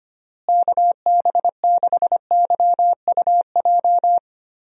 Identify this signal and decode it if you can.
Morse code, heard as a keyed tone, about 25 wpm, 700 Hz. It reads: KB6YUJ
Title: USA Call Signs - 6 Characters - Repeated 2x - 25wpm